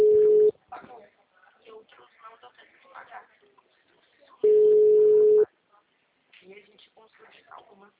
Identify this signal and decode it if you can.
Ringback tone (Europe, 425 Hz), with speech.